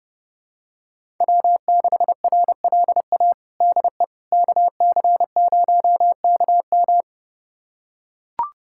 Morse code, heard as a keyed tone, about 30 wpm, 700 Hz. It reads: W6RLA DE KC0KM
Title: USA Callsign DE Callsign - 5 Characters - 30wpm